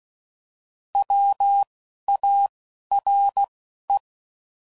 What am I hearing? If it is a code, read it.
Morse code, heard as a keyed tone, about 16 wpm, spaced as for 11 wpm, 800 Hz. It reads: WARE